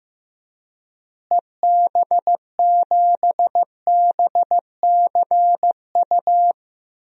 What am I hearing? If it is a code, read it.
Morse code, heard as a keyed tone, about 16 wpm, 700 Hz. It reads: EB7BCU
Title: DXCC-100 Call Signs - 6 Characters - Repeated 3x - 15wpm